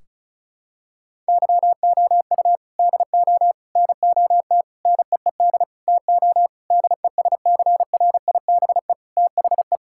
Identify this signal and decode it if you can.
Morse code, heard as a keyed tone, about 36 wpm, 700 Hz. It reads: YOU DO NOT NEED TO DESCRIBE THE
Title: Sentences from Top 1600 Words 35wpm